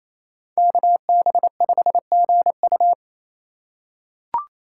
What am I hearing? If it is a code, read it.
Morse code, heard as a keyed tone, about 28 wpm, 700 Hz. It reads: KB5GU
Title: USA Call Signs - 2x Character Spacing - 28wpm